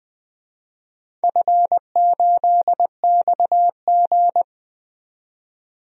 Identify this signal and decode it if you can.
Morse code, heard as a keyed tone, about 20 wpm, 700 Hz. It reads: F8XG
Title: DXCC-100 Call Signs - 4 Characters - 20wpm